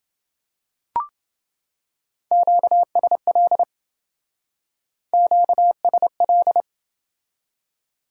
Morse code, heard as a keyed tone, about 27 wpm, 700 Hz. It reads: QSL QSL